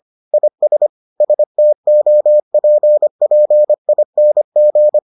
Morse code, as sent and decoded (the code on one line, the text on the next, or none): .. ... / ... - --- .--. .--. .. -. --.
IS STOPPING